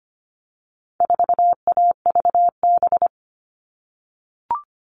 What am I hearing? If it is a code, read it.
Morse code, heard as a keyed tone, about 25 wpm, 700 Hz. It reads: 4AVB
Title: All Cut Numbers - 4-Digit 25wpm